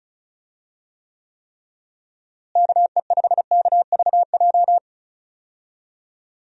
Morse code, heard as a keyed tone, about 34 wpm, 700 Hz. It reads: KE5KVJ
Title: USA Call Signs - 6 Characters - Repeated 3x - 35wpm